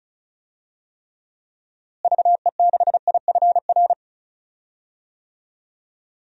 Morse code, heard as a keyed tone, about 36 wpm, 700 Hz. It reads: VE6IFR
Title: DXCC-100 Call Signs - 6 Characters - Repeated 3x - 35wpm